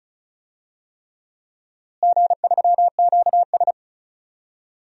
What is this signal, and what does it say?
Morse code, heard as a keyed tone, about 36 wpm, 700 Hz. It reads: G3QS
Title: DXCC-100 Call Signs - 4 Characters - Repeated 3x - 35wpm